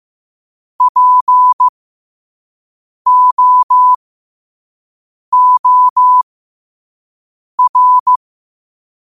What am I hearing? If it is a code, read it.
Morse code, heard as a keyed tone, about 15 wpm, spaced as for 5 wpm, 1000 Hz. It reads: POOR
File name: qotdfiles / 5wpm